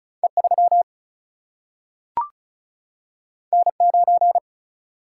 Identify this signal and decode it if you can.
Morse code, heard as a keyed tone, about 36 wpm, 700 Hz. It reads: E3 N9
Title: Single-Letter Single-Number 35wpm